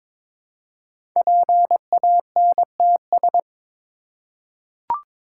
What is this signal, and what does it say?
Morse code, heard as a keyed tone, about 22 wpm, 700 Hz. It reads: PANTS
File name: 5 Letter Words - First 3 Letters Given - 22wpm